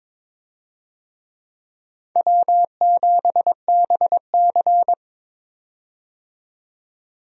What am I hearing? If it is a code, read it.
Morse code, heard as a keyed tone, about 22 wpm, 700 Hz. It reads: W7BC
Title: USA Call Signs - 4 Characters - 22wpm